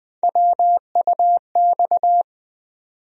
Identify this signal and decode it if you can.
Morse code, heard as a keyed tone, about 21 wpm, 700 Hz. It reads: WUX
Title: USA Call Sign Postfixes - 3 Letters 20wpm